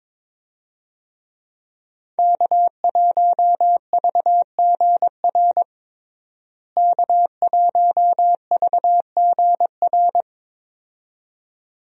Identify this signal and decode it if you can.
Morse code, heard as a keyed tone, about 22 wpm, 700 Hz. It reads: K1VGR K1VGR
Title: USA Call Signs - 5 Characters - Repeated 2x - 22wpm